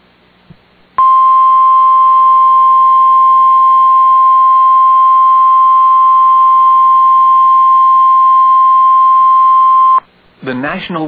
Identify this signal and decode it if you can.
Emergency Alert System (SAME) tones (the 1050 Hz attention tone), with speech. It is heard from 0.9 to 10.1 s.